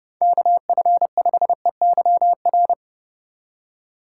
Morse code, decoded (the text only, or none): KF5EYR